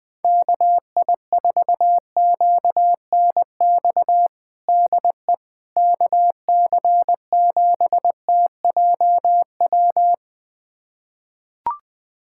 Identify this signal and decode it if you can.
Morse code, heard as a keyed tone, about 21 wpm, 700 Hz. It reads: KI4QNX DE KC7TJW
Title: USA Callsign DE Callsign - 6 Characters - 20wpm